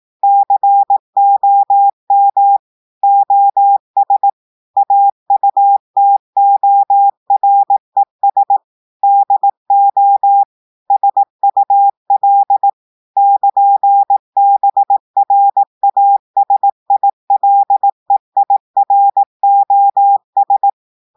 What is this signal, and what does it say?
Morse code, heard as a keyed tone, about 18 wpm, 800 Hz. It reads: COM OS AUTORES DO SUL <KN>BRASILEIROS
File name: audios / portugais / portugais texte en clair 1 vitesse 18 mots minutes